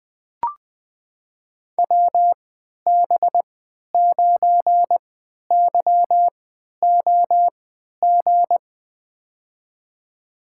Morse code, heard as a keyed tone, about 20 wpm, spaced as for 11 wpm, 700 Hz. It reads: WB9YOG
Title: USA Call Signs - 6 Characters - 3x Character Spacing - 20wpm